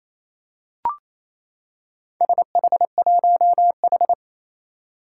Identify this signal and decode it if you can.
Morse code, heard as a keyed tone, about 28 wpm, 700 Hz. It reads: SH1H
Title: Four Challenging Letter-Number Combinations 28wpm